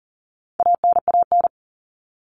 Morse code, heard as a keyed tone, about 40 wpm, 700 Hz. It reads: ANAN
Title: All Cut Numbers - 4-Digit 40wpm